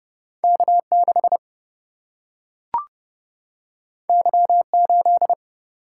Morse code, heard as a keyed tone, about 30 wpm, 700 Hz. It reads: K 6 Y 8
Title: Single-Letter Single-Number 30wpm